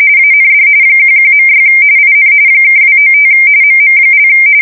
RTTY, about 45.45 baud. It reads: AND 11.6 IN (295 MM)
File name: fsk170 / USS-Missouri